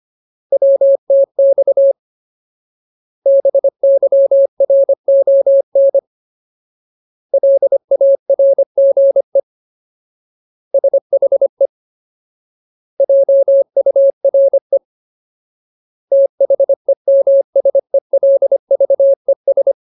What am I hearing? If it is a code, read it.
Morse code, heard as a keyed tone, about 25 wpm, 550 Hz. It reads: WTX BYRON LARGE SHE JURE THEMSELVES
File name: Ultimate Word List / 25WPM Ultimate